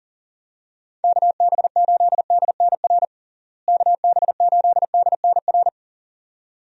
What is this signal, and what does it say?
Morse code, heard as a keyed tone, about 40 wpm, 700 Hz. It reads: KB8DNR KB8DNR
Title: USA Call Signs - 6 Characters - Repeated 2x - 40wpm